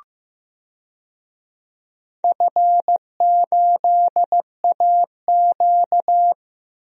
Morse code, heard as a keyed tone, about 15 wpm, 700 Hz. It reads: F8AQ
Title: DXCC-100 Call Signs - 4 Characters - 15wpm